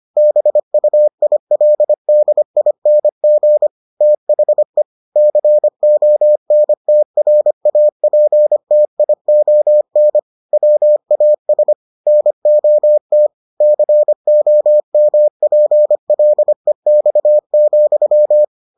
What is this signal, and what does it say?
Morse code, heard as a keyed tone, about 25 wpm, 600 Hz. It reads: BUILDING THE CONTRAPTION WAS NOT COMPLEX,